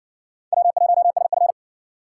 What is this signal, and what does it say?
Morse code, heard as a keyed tone, about 53 wpm, 700 Hz. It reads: W1SF